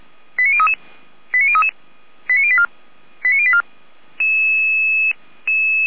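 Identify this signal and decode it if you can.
Five-tone selective call (ZVEI), heard as quick runs of five tones, tones of about 70 ms. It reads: 89033 89033 89084 89084
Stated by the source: ZVEI five-tone call received on SDR, feuerwehr-orlamuende.org fire brigade.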